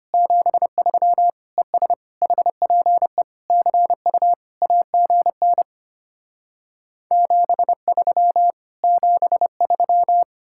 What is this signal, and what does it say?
Morse code, heard as a keyed tone, about 33 wpm, 700 Hz. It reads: 73 ES HPE CU AGN 73 73